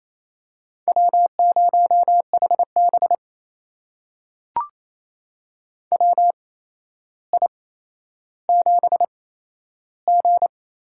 Morse code, heard as a keyed tone, about 28 wpm, spaced as for 12 wpm, 700 Hz. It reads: W0HB WI7G
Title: USA Call Signs - 4 Characters - 8x Character Spacing - 28wpm